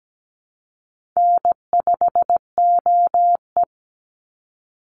Morse code, heard as a keyed tone, about 17 wpm, 700 Hz. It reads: N5OE
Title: Easy Call Signs Repeated 2x - 17wpm